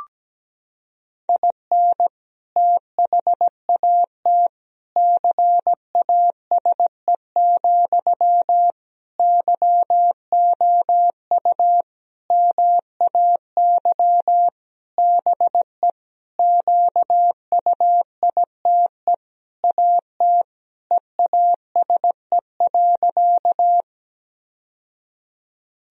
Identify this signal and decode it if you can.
Morse code, heard as a keyed tone, about 17 wpm, 700 Hz. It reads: IN THAT CASE, YOU MAY BE QUITE AT EASE.
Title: Sentences from Top 1000 Words - Review Part 1 - 17wpm